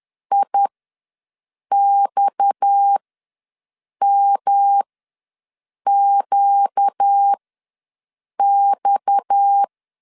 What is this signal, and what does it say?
Morse code, heard as a keyed tone, about 10 wpm, spaced as for 6 wpm, 800 Hz. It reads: IXMQX
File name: LEZIONI / 6 PKX0 / 6 PKX ws 10wpm 3706